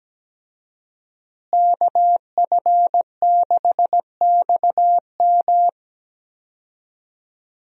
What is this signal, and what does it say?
Morse code, heard as a keyed tone, about 17 wpm, 700 Hz. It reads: KF6XM
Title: USA Call Signs - 5 Characters - Repeated 3x - 17wpm